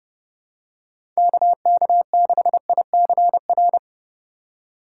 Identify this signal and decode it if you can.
Morse code, heard as a keyed tone, about 30 wpm, 700 Hz. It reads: KK6ICR